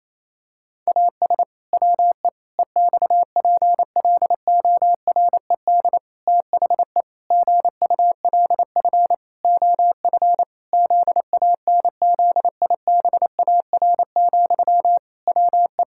Morse code, heard as a keyed tone, about 28 wpm, 700 Hz. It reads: AS WE EXPLORED THE GULF OF ZANZIBAR, WE
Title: Pangram Challenge v2 - 28wpm